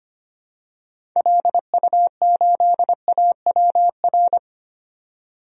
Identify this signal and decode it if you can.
Morse code, heard as a keyed tone, about 25 wpm, 700 Hz. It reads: LU8AWR
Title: DXCC-100 Call Signs - 6 Characters - 25wpm